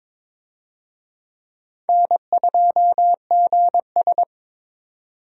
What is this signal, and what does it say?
Morse code, heard as a keyed tone, about 22 wpm, 700 Hz. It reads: N2GS
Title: USA Call Signs - 4 Characters - 22wpm